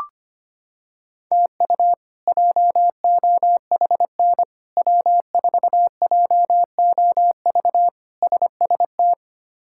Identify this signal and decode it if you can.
Morse code, heard as a keyed tone, about 25 wpm, 700 Hz. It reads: TU JOHN W4JOV SST
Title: K1USN SST QSO Practice - 25wpm